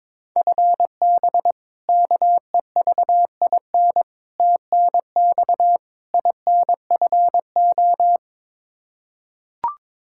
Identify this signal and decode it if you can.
Morse code, heard as a keyed tone, about 22 wpm, 700 Hz. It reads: FB KEVIN TNX INFO
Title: Practice QSO - Exchange 3 - More Info - 22wpm